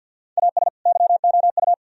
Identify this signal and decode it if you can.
Morse code, heard as a keyed tone, about 62 wpm, 700 Hz. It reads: AS YOU
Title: Speed-Racing - Sets of 2 Words 50wpm